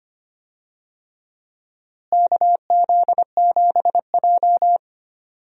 Morse code, heard as a keyed tone, about 25 wpm, 700 Hz. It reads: KZ7J
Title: USA Call Signs - 4 Characters - Repeated 3x - 25wpm